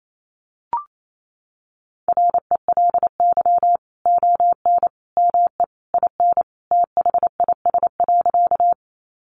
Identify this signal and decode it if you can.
Morse code, heard as a keyed tone, about 28 wpm, 700 Hz. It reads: RELY ON ME IN THIS.
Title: Sentences from Top 1200 Words 28wpm